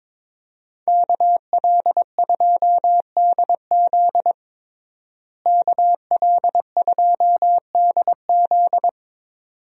Morse code, heard as a keyed tone, about 22 wpm, 700 Hz. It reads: KL2DZ KL2DZ
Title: USA Call Signs - 5 Characters - Repeated 2x - 22wpm